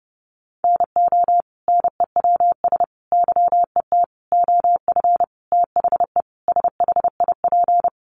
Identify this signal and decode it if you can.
Morse code, heard as a keyed tone, about 30 wpm, 700 Hz. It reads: NO NEWS YET OF THE SHIP
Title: Sentences from Top 800 Words 30wpm